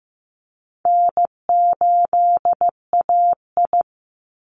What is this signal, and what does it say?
Morse code, heard as a keyed tone, about 15 wpm, 700 Hz. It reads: N8AI
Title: Call Signs Repeated 2x - 15wpm